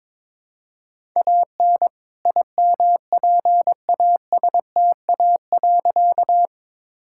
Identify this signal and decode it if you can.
Morse code, heard as a keyed tone, about 22 wpm, 700 Hz. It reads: AN IMPASTA.
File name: All-Age Appropriate Jokes - 22wpm